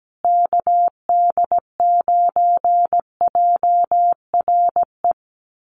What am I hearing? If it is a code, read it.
Morse code, heard as a keyed tone, about 17 wpm, 700 Hz. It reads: KD9JRE
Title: USA Call Signs Repeated 2x 17wpm